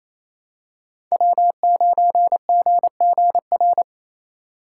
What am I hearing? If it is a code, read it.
Morse code, heard as a keyed tone, about 28 wpm, 700 Hz. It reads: W9GGR
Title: USA Call Signs - 5 Characters - 6x Character Spacing - 28wpm